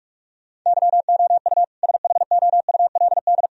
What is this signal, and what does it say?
Morse code, heard as a keyed tone, about 46 wpm, 700 Hz. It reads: YOU SHOULD
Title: Sentences from Top 1000 Words - Review Part 3 - 45wpm